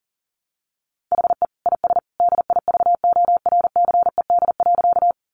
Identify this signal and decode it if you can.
Morse code, heard as a keyed tone, about 40 wpm, 700 Hz. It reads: HE IS DIVORCED.
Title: Additional Easy Phrases 40wpm